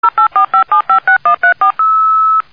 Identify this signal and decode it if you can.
DTMF, heard as telephone touch tones, tones of about 110 ms. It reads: 0845756134